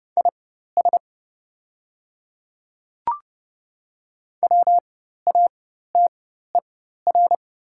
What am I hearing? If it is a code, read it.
Morse code, heard as a keyed tone, about 30 wpm, spaced as for 14 wpm, 700 Hz. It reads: IS WATER
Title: 4x Character Spacing - Top 300 Words - 30wpm